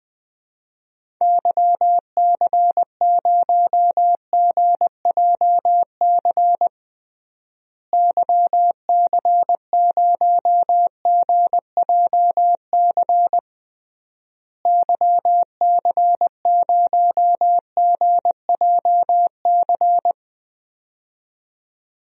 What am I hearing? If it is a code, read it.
Morse code, heard as a keyed tone, about 20 wpm, 700 Hz. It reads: YC0GJC YC0GJC YC0GJC
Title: DXCC-100 Call Signs - 6 Characters - Repeated 3x - 20wpm